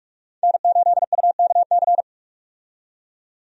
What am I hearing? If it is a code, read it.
Morse code, heard as a keyed tone, about 46 wpm, 700 Hz. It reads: N8UKC
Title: USA Call Signs - 5 Characters - 45wpm